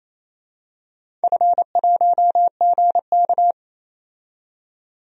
Morse code, heard as a keyed tone, about 28 wpm, 700 Hz. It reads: F1GK
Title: DXCC-100 Call Signs - 4 Characters - 28wpm